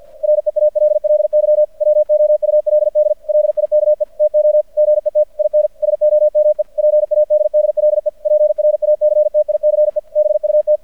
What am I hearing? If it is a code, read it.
Morse code, heard as a keyed tone, about 50 wpm, 600 Hz. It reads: REALLY WOULD LIKE TO GET IN SOME PADDLE PRACTICE BUT
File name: archived / CRSnet